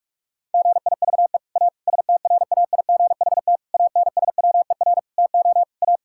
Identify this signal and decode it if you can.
Morse code, heard as a keyed tone, about 46 wpm, 700 Hz. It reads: GIVE A STRAIGHT ANSWER TO A
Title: Sentences from Top 1000 Words - Review Part 6 - 45wpm